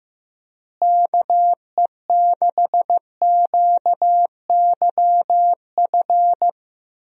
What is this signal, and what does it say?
Morse code, heard as a keyed tone, about 15 wpm, 700 Hz. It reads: KE6QYF
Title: USA Call Signs - 6 Characters - Repeated 3x - 15wpm